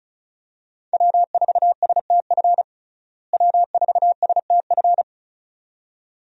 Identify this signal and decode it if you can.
Morse code, heard as a keyed tone, about 34 wpm, 700 Hz. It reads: W4STF W4STF
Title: USA Call Signs - 5 Characters - Repeated 2x - 35wpm